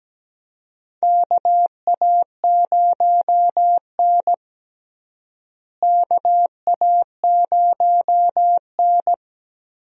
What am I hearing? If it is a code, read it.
Morse code, heard as a keyed tone, about 17 wpm, 700 Hz. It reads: KA0N KA0N
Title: USA Call Signs - 4 Characters - Repeated 3x - 17wpm